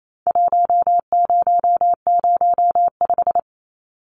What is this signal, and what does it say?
Morse code, heard as a keyed tone, about 28 wpm, 700 Hz. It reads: 1005
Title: Numbers - Contest Style - 4-Digit 28wpm